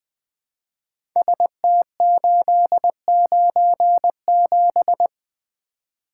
Morse code, heard as a keyed tone, about 20 wpm, 700 Hz. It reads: ST897
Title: Code Groups - 5 Letter:Numbers - 20wpm